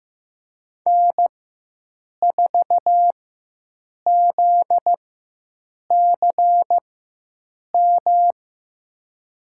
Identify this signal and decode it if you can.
Morse code, heard as a keyed tone, about 15 wpm, spaced as for 7 wpm, 700 Hz. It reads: N4ZCM
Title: USA Call Signs - 5 Characters - 4x Character Spacing - 15wpm